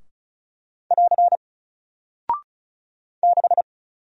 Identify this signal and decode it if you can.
Morse code, heard as a keyed tone, about 36 wpm, 700 Hz. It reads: <AR> 6
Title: Single Character (Letter:Number:Punctuation:Prosign) 35wpm